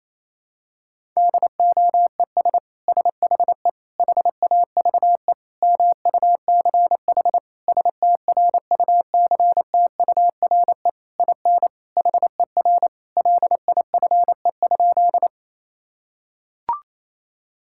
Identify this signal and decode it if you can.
Morse code, heard as a keyed tone, about 28 wpm, 700 Hz. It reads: DOES SHE HAVE MUCH STRUCTURE IN HER LIFE?
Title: Sentences from Top 1000 Words - Review Part 6 - 28wpm